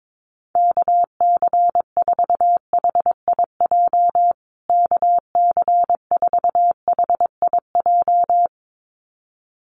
Morse code, heard as a keyed tone, about 22 wpm, 700 Hz. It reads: KC4HIJ KC4HIJ
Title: USA Call Signs Repeated 2x 22wpm